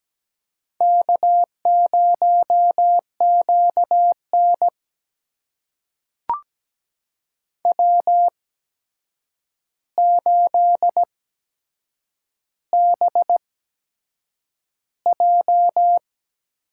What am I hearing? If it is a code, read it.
Morse code, heard as a keyed tone, about 17 wpm, spaced as for 7 wpm, 700 Hz. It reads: K0QN W8BJ
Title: USA Call Signs - 4 Characters - 8x Character Spacing - 17wpm